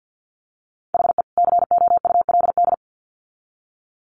Morse code, heard as a keyed tone, about 50 wpm, 700 Hz. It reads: HE COULD
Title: Sets of 2 Words 50wpm